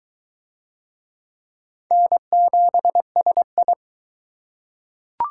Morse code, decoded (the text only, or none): N7SI